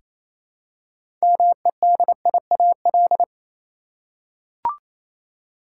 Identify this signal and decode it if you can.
Morse code, heard as a keyed tone, about 27 wpm, 700 Hz. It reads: MEDIAL
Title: K1USN SST QSO Practice 28wpm